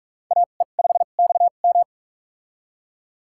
Morse code, heard as a keyed tone, about 46 wpm, spaced as for 33 wpm, 700 Hz. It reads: AE5XM
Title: USA Call Signs - 2x Character Spacing - 45wpm